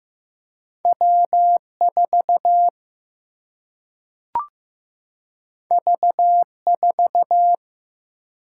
Morse code, heard as a keyed tone, about 15 wpm, 700 Hz. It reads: W 4 V 4
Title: Single-Letter Single-Number 15wpm